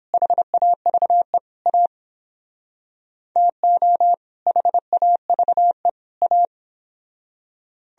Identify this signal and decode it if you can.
Morse code, heard as a keyed tone, about 29 wpm, 700 Hz. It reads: HAVE A TO HAVE A